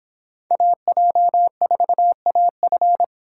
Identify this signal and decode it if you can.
Morse code, heard as a keyed tone, about 26 wpm, 700 Hz. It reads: AJ4AF